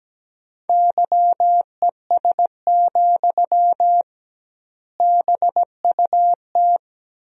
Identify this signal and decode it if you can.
Morse code, heard as a keyed tone, about 17 wpm, 700 Hz. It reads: YES, BUT
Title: 2x Word-Spacing - Sentences from Top 500 Words - 17wpm